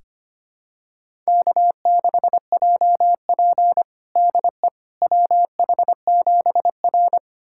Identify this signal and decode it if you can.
Morse code, heard as a keyed tone, about 25 wpm, 700 Hz. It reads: K6JP DE WH7R
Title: USA Callsign DE Callsign - 4 Characters - 25wpm